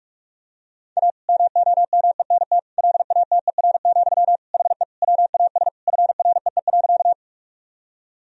Morse code, heard as a keyed tone, about 46 wpm, 700 Hz. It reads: A MOMENT LATER, HE WAS FREE.
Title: Sentences from Top 1000 Words - Review Part 4 - 45wpm